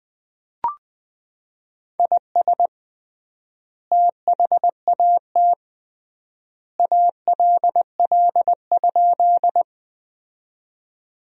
Morse code, heard as a keyed tone, about 20 wpm, 700 Hz. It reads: IS THAT ALL?
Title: 3x Word-Spacing - Sentences from Top 100 Words - 20wpm